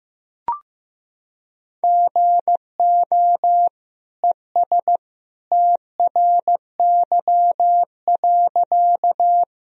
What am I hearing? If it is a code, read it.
Morse code, heard as a keyed tone, about 16 wpm, 700 Hz. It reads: GO ES TRY.
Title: Sentences from Top 200 Words and Abbreviations 15wpm